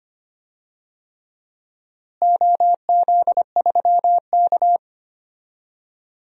Morse code, heard as a keyed tone, about 25 wpm, 700 Hz. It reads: OZ3K
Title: DXCC-100 Call Signs - 4 Characters - Repeated 3x - 25wpm